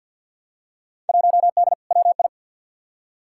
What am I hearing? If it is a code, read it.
Morse code, heard as a keyed tone, about 53 wpm, 700 Hz. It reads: 1D WI
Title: Field Day Exchanges - 50wpm